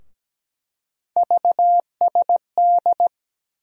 Morse code, heard as a keyed tone, about 17 wpm, 700 Hz. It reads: VSD
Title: USA Call Sign Postfixes - 3 Letters 17wpm